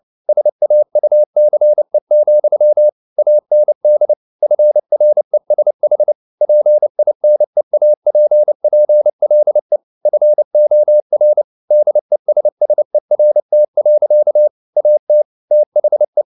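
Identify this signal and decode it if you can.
Morse code, heard as a keyed tone, about 29 wpm, 600 Hz. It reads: SAUCE, AND FRESH PINEAPPLE FOR DESSERT. AT THE
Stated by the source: Morse code at 29 wpm, sending Ken Follett's novel Never in English, from CW med Gnister.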